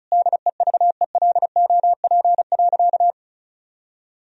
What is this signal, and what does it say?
Morse code, heard as a keyed tone, about 34 wpm, 700 Hz. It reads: DEVELOP.
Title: Sentences from Top 1000 Words - Review Part 2 - 35wpm